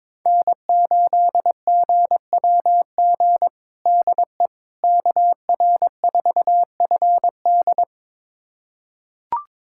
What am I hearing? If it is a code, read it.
Morse code, heard as a keyed tone, about 22 wpm, 700 Hz. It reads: N8GWG DE KR4FD
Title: USA Callsign DE Callsign - 5 Characters - 22wpm